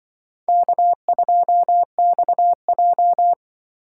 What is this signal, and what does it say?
Morse code, heard as a keyed tone, about 25 wpm, 700 Hz. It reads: K2XJ